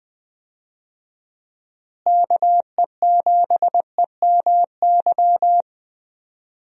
Morse code, heard as a keyed tone, about 20 wpm, 700 Hz. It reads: KE7EMY
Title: USA Call Signs - 6 Characters - Repeated 3x - 20wpm